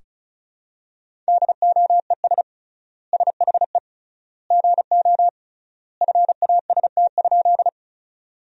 Morse code, heard as a keyed tone, about 36 wpm, 700 Hz. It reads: DOES SHE GO FAST?
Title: 3x Word-Spacing - Sentences from Top 400 Words - 35wpm